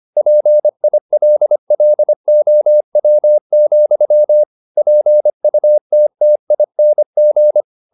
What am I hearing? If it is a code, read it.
Morse code, heard as a keyed tone, about 25 wpm, 600 Hz. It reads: PILLOW, PUTTING
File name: the old man and the sea part 1 25wpm